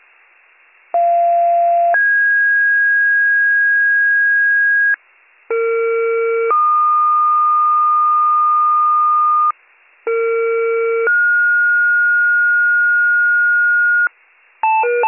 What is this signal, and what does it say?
Two-tone page, heard as a one-second tone, then a three-second tone, about 690 then 1650 Hz, about 460 then 1220 Hz, about 460 then 1430 Hz, with noise. It is heard from 0.9 to 14.1 s.